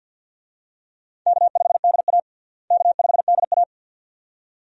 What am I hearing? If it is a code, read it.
Morse code, heard as a keyed tone, about 52 wpm, 700 Hz. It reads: K5DA K5DA
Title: USA Call Signs - 4 Characters - Repeated 2x - 50wpm